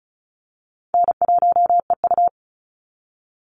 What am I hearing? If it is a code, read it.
Morse code, heard as a keyed tone, about 34 wpm, 700 Hz. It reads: N1EU
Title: NA SOTA Chaser Callsigns 35wpm